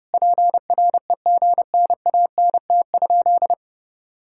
Morse code, heard as a keyed tone, about 28 wpm, 700 Hz. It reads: PREGNANT?
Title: Sentences from Top 1000 Words - Review Part 6 - 30wpm